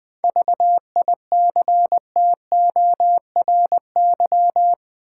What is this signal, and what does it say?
Morse code, heard as a keyed tone, about 20 wpm, 700 Hz. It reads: VICTORY
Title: 7 Letter Words - First 3 Letters Given - 20wpm